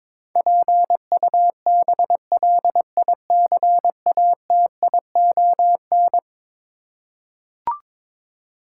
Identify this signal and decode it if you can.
Morse code, heard as a keyed tone, about 22 wpm, 700 Hz. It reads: PUBLICATION
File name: practice / 22wpm / Eleven Letter Words - 22wpm